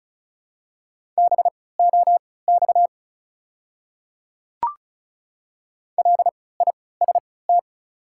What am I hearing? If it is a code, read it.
Morse code, heard as a keyed tone, about 36 wpm, spaced as for 20 wpm, 700 Hz. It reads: BOX LIST
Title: 3x Character Spacing - Top 500 Words - 35wpm